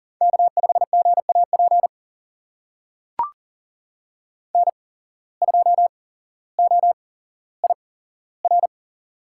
Morse code, heard as a keyed tone, about 40 wpm, spaced as for 17 wpm, 700 Hz. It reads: K5GAP N2OIR
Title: USA Call Signs - 5 Characters - 8x Character Spacing - 40wpm